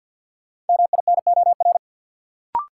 Morse code, heard as a keyed tone, about 50 wpm, 700 Hz. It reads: MINOR
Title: Sentences from Top 1500 Words - New Word Drill Alt - 50wpm